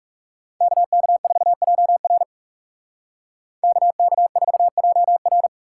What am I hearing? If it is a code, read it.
Morse code, heard as a keyed tone, about 39 wpm, 700 Hz. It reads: KK4JR KK4JR